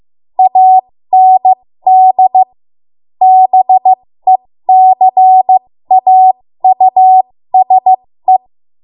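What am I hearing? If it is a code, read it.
Morse code, heard as a keyed tone, about 15 wpm, spaced as for 13 wpm, 750 Hz. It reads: AND BECAUSE
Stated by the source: 13 wpm Morse code practice archive transmission.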